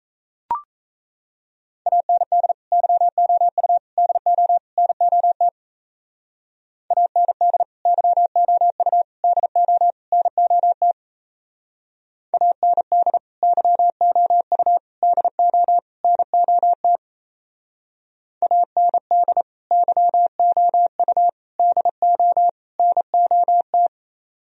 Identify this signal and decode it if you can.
Morse code, heard as a keyed tone, about 35 wpm, 700 Hz. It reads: AND YOU DO NOT AND YOU DO NOT AND YOU DO NOT AND YOU DO NOT